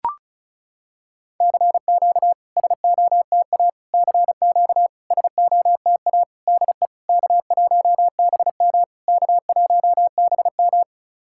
Morse code, heard as a keyed tone, about 34 wpm, 690 Hz. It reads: CQ SOTA CQ SOTA DE K1BM K1BM
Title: SOTA QSO Practice 35wpm